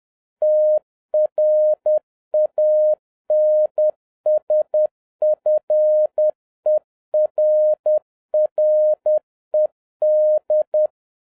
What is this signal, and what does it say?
Morse code, heard as a keyed tone, about 10 wpm, 610 Hz. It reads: TRANSFERRED